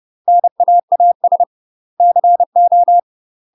Morse code, heard as a keyed tone, about 30 wpm, 700 Hz. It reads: NAAS CO
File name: UK MASTER QSOs TRAINING / 30WPM / UK MASTER QSOs 0001